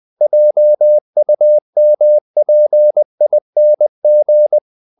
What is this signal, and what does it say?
Morse code, heard as a keyed tone, about 20 wpm, 600 Hz. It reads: JUMPING
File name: the old man and the sea / the old man and the sea part 1 20wpm